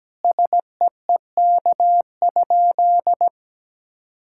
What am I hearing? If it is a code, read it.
Morse code, heard as a keyed tone, about 17 wpm, 700 Hz. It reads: SEEK?
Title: Sentences from Top 1600 Words 17wpm